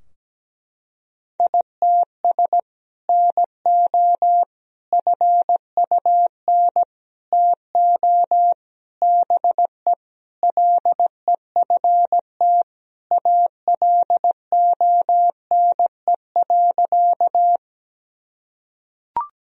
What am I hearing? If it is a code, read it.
Morse code, heard as a keyed tone, about 17 wpm, 700 Hz. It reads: ITS NO FUN TO BE LEFT ALONE.
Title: Sentences from Top 1000 Words - Review Part 4 - 17wpm